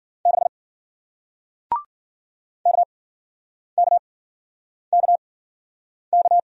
Morse code, heard as a keyed tone, about 56 wpm, spaced as for 9 wpm, 700 Hz. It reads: 6 KKKK